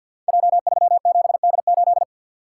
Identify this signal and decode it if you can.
Morse code, heard as a keyed tone, about 50 wpm, 700 Hz. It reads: J37D8
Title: Code Groups - 5 Letter:Numbers - 50wpm